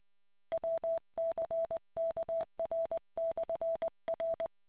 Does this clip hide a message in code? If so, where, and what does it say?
Yes: Morse code from 0.5 to 4.5 s, heard as a keyed tone, about 20 wpm, 660 Hz. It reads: WCKR/R